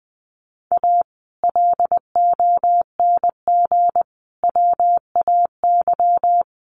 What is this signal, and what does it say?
Morse code, heard as a keyed tone, about 20 wpm, 700 Hz. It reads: A LONG WAY
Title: Sets of 5 Words 20wpm